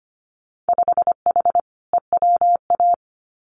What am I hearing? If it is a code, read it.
Morse code, heard as a keyed tone, about 25 wpm, 700 Hz. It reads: 5H EWA
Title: Winter Field Day Exchanges Encore 25wpm